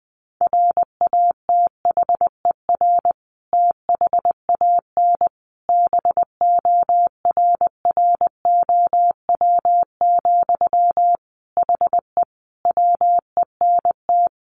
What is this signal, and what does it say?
Morse code, heard as a keyed tone, about 20 wpm, 700 Hz. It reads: RATHER THAN BORROW, HE WENT
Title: Sentences from Top 1500 Words 20wpm